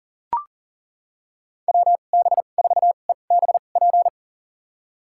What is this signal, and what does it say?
Morse code, heard as a keyed tone, about 40 wpm, spaced as for 29 wpm, 700 Hz. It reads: WB4EBP
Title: USA Call Signs - 2x Character Spacing - 40wpm